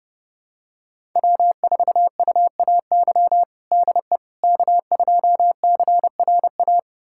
Morse code, heard as a keyed tone, about 30 wpm, 700 Hz. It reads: W4UAY DE K2CRA
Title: USA Callsign DE Callsign - 5 Characters - 30wpm